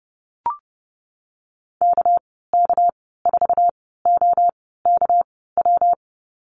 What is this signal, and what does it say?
Morse code, heard as a keyed tone, about 30 wpm, spaced as for 17 wpm, 700 Hz. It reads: KK4OKW
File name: Lesson 37g - Letter x - Call Signs 30wpm